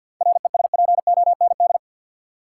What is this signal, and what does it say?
Morse code, heard as a keyed tone, about 49 wpm, 700 Hz. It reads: RESPOND
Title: Seven Letter Words - 50wpm